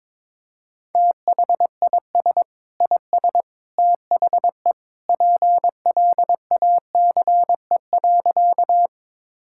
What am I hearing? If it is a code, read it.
Morse code, heard as a keyed tone, about 21 wpm, 700 Hz. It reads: THIS IS THE PLACE.
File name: practice / 22wpm / Sentences with 3-Word Phrases - 22wpm